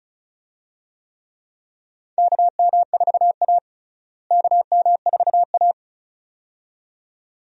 Morse code, heard as a keyed tone, about 34 wpm, 700 Hz. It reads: KM4A KM4A
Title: USA Call Signs - 4 Characters - Repeated 2x - 35wpm